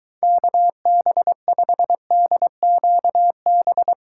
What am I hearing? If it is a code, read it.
Morse code, heard as a keyed tone, about 23 wpm, 700 Hz. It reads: KB5DQB